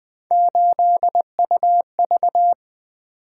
Morse code, heard as a keyed tone, about 20 wpm, 700 Hz. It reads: 8UV
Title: Three Letter-Number Combinations - 20wpm